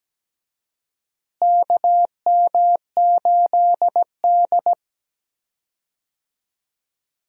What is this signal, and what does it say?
Morse code, heard as a keyed tone, about 17 wpm, 700 Hz. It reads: KM8D
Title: USA Call Signs - 4 Characters - Repeated 2x - 17wpm